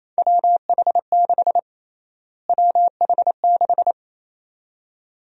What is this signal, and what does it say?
Morse code, heard as a keyed tone, about 28 wpm, 700 Hz. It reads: WH6 WH6
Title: USA Call Sign Prefixes - Repeated 2x - 28wpm